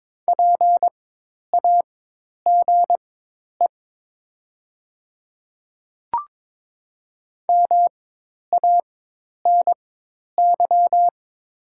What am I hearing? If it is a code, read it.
Morse code, heard as a keyed tone, about 22 wpm, spaced as for 10 wpm, 700 Hz. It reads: PAGE MANY
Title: 4x Character Spacing - Top 300 Words - 22wpm